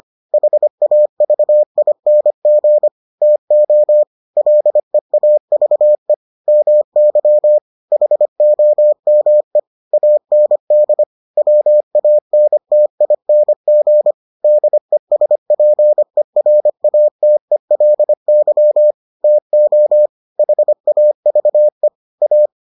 Morse code, heard as a keyed tone, about 25 wpm, 600 Hz. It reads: HAVING TO LEAVE MY HOME AND WANTING DESPERATELY TO HAVE A